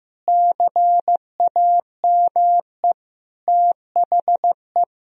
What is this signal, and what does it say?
Morse code, heard as a keyed tone, about 16 wpm, 700 Hz. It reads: CAME THE
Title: Sentences from Top 1000 Words - Review Part 1 - 15wpm